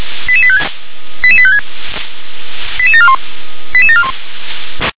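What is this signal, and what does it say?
Five-tone selective call (ZVEI), heard as quick runs of five tones, tones of about 70 ms. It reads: 88856 88856 88741 88741
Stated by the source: ZVEI five-tone call 88769 received by SDR.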